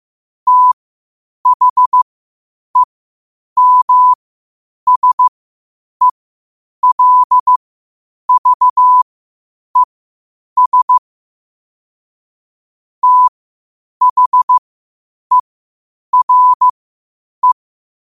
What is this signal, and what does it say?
Morse code, heard as a keyed tone, about 15 wpm, spaced as for 8 wpm, 1000 Hz. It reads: THEMSELVES THERE
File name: qotdfiles / 8wpm